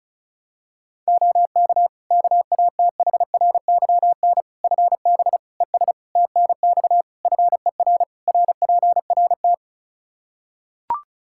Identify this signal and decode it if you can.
Morse code, heard as a keyed tone, about 34 wpm, 700 Hz. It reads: OK KATHRYN FB ES TNX FER RPRT
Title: K1USN SST QSO Practice 35wpm